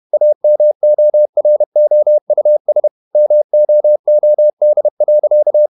Morse code, heard as a keyed tone, about 31 wpm, 600 Hz. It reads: AMOROUS MOOD.
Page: Never 31wpm | CW med Gnister